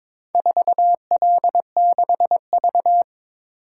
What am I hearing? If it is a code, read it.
Morse code, heard as a keyed tone, about 23 wpm, 700 Hz. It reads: 4L6V
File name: practice / 22wpm / Four Letter-Number Combinations - 22wpm